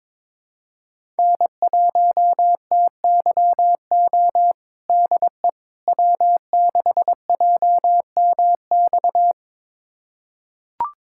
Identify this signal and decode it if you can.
Morse code, heard as a keyed tone, about 22 wpm, 700 Hz. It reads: N1TYO DE W6JMX
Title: USA Callsign DE Callsign - 5 Characters - 22wpm